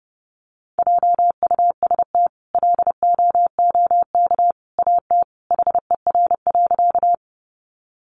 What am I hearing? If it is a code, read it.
Morse code, heard as a keyed tone, about 30 wpm, 700 Hz. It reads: JUST LOOK AT HER.
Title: Sentences from Top 200 Words 30wpm